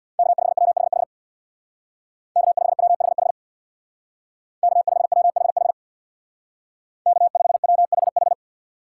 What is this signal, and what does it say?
Morse code, heard as a keyed tone, about 61 wpm, 700 Hz. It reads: K5WHH K5WHH K5WHH K5WHH
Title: Speed Racing - USA 1x3 Call Signs 50wpm